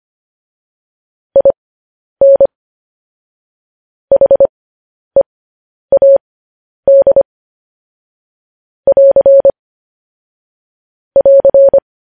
Morse code, heard as a keyed tone, about 25 wpm, spaced as for 10 wpm, 550 Hz. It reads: IN HEAD <AR> <AR>
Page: news-headlines-in-morse-code-at-10-wpm